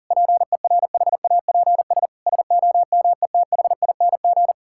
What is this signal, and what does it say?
Morse code, heard as a keyed tone, about 37 wpm, 700 Hz. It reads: PERHAPS SOMETHING
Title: Sentences from Top 1600 Words 40wpm